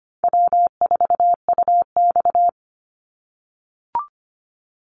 Morse code, heard as a keyed tone, about 25 wpm, 700 Hz. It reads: W4UX
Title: Easy Call Signs - 25wpm